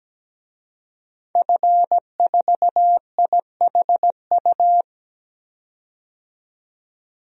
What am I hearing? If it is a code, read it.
Morse code, heard as a keyed tone, about 17 wpm, 700 Hz. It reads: F4IHU
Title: DXCC-100 Call Signs - 5 Characters - 17wpm